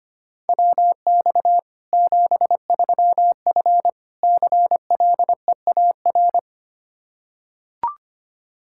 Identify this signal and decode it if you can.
Morse code, heard as a keyed tone, about 25 wpm, 700 Hz. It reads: WX 73F CLEAR
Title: Practice QSO - Exchange 2 - Info - 25wpm